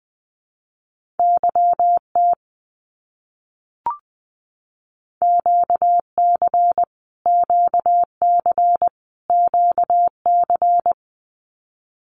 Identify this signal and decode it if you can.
Morse code, heard as a keyed tone, about 20 wpm, 700 Hz. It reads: YT QC QC QC
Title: Canadian Provinces and Territories - Repeated 3x - 20wpm